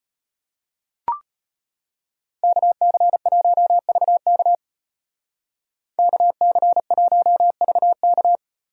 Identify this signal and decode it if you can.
Morse code, heard as a keyed tone, about 33 wpm, 700 Hz. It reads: KC1VK KC1VK